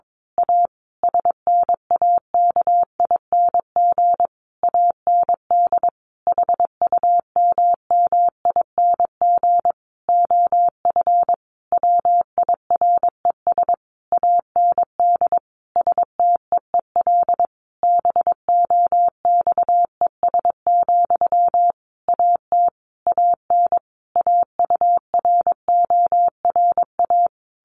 Morse code, heard as a keyed tone, about 22 wpm, 700 Hz. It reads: A SNAKING AND HUMMING OF WIRES AND STEEL BOXES, AT AN AURORA
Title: A Sound Of Thunder 22wpm